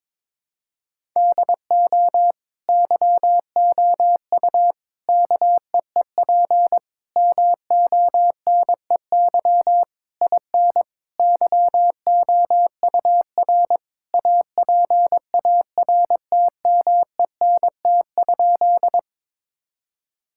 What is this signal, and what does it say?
Morse code, heard as a keyed tone, about 22 wpm, 700 Hz. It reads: DO YOU KEEP MONEY IN YOUR APARTMENT?
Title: Sentences from Top 1000 Words - Review Part 6 - 22wpm